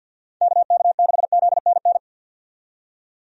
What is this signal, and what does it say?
Morse code, heard as a keyed tone, about 52 wpm, 700 Hz. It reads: KK6ZNN
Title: USA Call Signs - 6 Characters - 50wpm